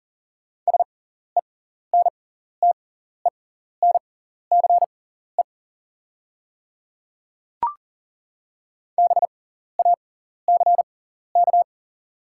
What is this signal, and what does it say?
Morse code, heard as a keyed tone, about 39 wpm, spaced as for 14 wpm, 700 Hz. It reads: SENTENCE BACK
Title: 6x Character Spacing - Top 200 Words - 40wpm